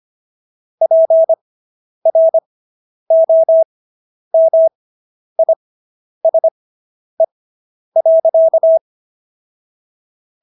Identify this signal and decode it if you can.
Morse code, heard as a keyed tone, about 25 wpm, spaced as for 10 wpm, 650 Hz. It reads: PROMISE.